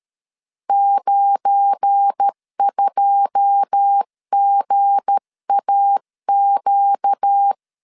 Morse code, heard as a keyed tone, about 13 wpm, 800 Hz. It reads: 92GAQ
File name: LEZIONI / 9 286 12wpm 2305